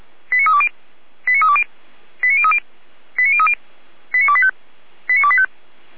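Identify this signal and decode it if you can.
Five-tone selective call (ZVEI), heard as quick runs of five tones, tones of about 70 ms. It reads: 89320 89320 89030 89030 89285 89285
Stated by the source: ZVEI five-tone selective call received by SDR.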